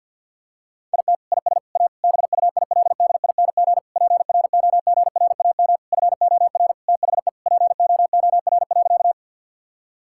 Morse code, heard as a keyed tone, about 49 wpm, 700 Hz. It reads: IT IS A BUILDING PROGRAM FOR THE POOR.
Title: Sentences from Top 1000 Words - Review Part 5 - 50wpm